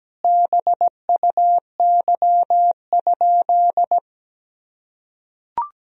Morse code, heard as a keyed tone, about 17 wpm, 700 Hz. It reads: BUY?
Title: Sentences from Top 1000 Words - Review Part 4 - 17wpm